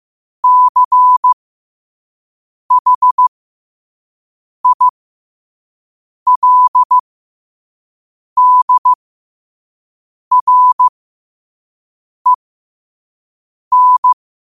Morse code, heard as a keyed tone, about 15 wpm, spaced as for 5 wpm, 1000 Hz. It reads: CHILDREN
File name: qotdfiles / 5wpm